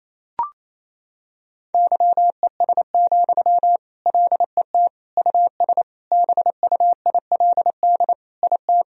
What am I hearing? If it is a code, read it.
Morse code, heard as a keyed tone, about 28 wpm, 700 Hz. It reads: YES, LET US BUILD IT
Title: Sentences from Top 1000 Words - Review Part 1 - 28wpm